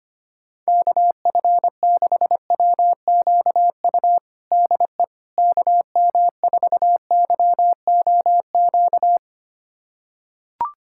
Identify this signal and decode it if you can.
Morse code, heard as a keyed tone, about 25 wpm, 700 Hz. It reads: KF6WQU DE KM4YOQ
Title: USA Callsign DE Callsign - 6 Characters - 25wpm